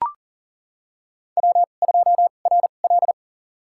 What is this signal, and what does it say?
Morse code, heard as a keyed tone, about 39 wpm, spaced as for 28 wpm, 690 Hz. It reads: W2RL
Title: USA Call Signs - 4 Characters - 2x Character Spacing - 40wpm